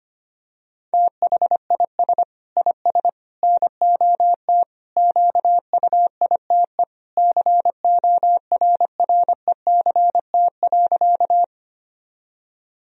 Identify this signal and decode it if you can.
Morse code, heard as a keyed tone, about 25 wpm, 700 Hz. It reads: THIS IS NOT QUITE CORRECT.
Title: Sentences from Top 1000 Words - Review Part 2 - 25wpm